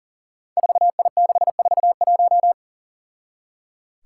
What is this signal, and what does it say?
Morse code, heard as a keyed tone, about 39 wpm, 700 Hz. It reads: 4I641
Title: Five Challenging Letter-Number Combinations 40wpm